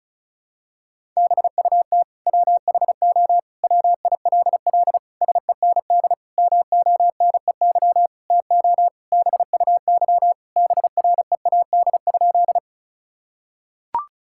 Morse code, heard as a keyed tone, about 36 wpm, 700 Hz. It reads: BUT WHO WILL SEND MONEY TO BUY BREAD?
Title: Sentences from Top 1600 Words 35wpm